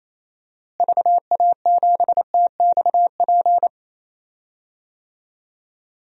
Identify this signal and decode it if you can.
Morse code, heard as a keyed tone, about 28 wpm, 700 Hz. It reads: VA7TXP
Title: DXCC-100 Call Signs - 6 Characters - Repeated 2x - 28wpm